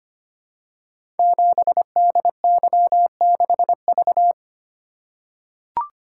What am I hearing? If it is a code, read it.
Morse code, heard as a keyed tone, about 25 wpm, 700 Hz. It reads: 7DY6V
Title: Code Groups - 5 Letter:Numbers - 25wpm